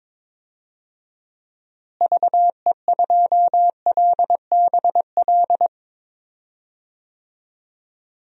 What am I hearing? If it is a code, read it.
Morse code, heard as a keyed tone, about 22 wpm, 700 Hz. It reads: VE2LBL
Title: DXCC-100 Call Signs - 6 Characters - Repeated 3x - 22wpm